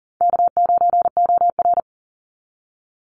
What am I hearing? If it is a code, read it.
Morse code, heard as a keyed tone, about 40 wpm, 700 Hz. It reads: K9OR
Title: Easy Call Signs Repeated 3x - 40wpm